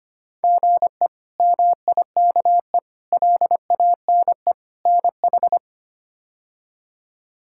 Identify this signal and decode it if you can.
Morse code, heard as a keyed tone, about 25 wpm, 700 Hz. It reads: GE MIKE LANE NH
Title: K1USN SST QSO Practice - 25wpm